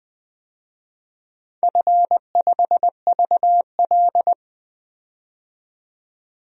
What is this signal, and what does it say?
Morse code, heard as a keyed tone, about 20 wpm, 700 Hz. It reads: F5VL
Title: DXCC-100 Call Signs - 4 Characters - 20wpm